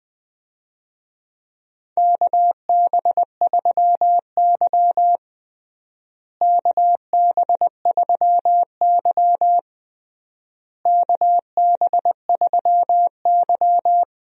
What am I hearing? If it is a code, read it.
Morse code, heard as a keyed tone, about 20 wpm, 700 Hz. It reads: KB3Y KB3Y KB3Y
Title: USA Call Signs - 4 Characters - Repeated 3x - 20wpm